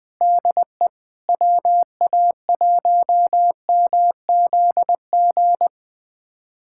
Morse code, heard as a keyed tone, about 21 wpm, 700 Hz. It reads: DE WA1MZG
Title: USA Callsign DE Callsign 20wpm